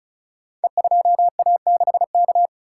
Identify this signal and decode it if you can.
Morse code, heard as a keyed tone, about 36 wpm, 700 Hz. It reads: E2A6K
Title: Code Groups - 5 Letter:Numbers - 35wpm